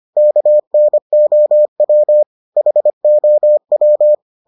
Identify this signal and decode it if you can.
Morse code, heard as a keyed tone, about 25 wpm, 600 Hz. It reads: KNOW HOW